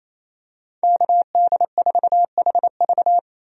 Morse code, heard as a keyed tone, about 28 wpm, 700 Hz. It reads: KD4HV
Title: USA Call Signs - 5 Characters - 28wpm